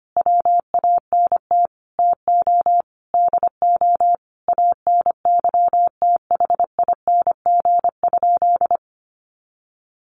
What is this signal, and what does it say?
Morse code, heard as a keyed tone, about 25 wpm, 700 Hz. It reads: WANT TO DO ANYTHING?
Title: Sentences from Top 1000 Words 25wpm